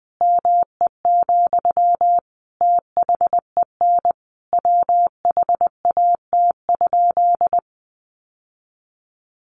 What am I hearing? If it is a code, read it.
Morse code, heard as a keyed tone, about 20 wpm, 700 Hz. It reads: ME, THEN WHAT?
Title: Sentences from Top 1300 Words 20wpm